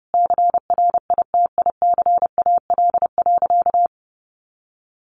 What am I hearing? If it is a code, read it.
Morse code, heard as a keyed tone, about 30 wpm, 700 Hz. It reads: CRITICAL.
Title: Sentences from Top 1100 Words 30wpm